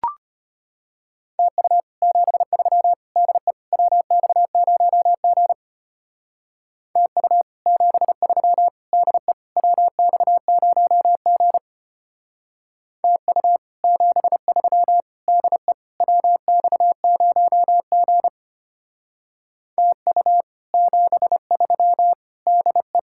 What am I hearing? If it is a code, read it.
Morse code, heard as a keyed tone, about 32 wpm, 700 Hz. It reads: TU 73 DE WX0G TU 73 DE WX0G TU 73 DE WX0G TU 73 DE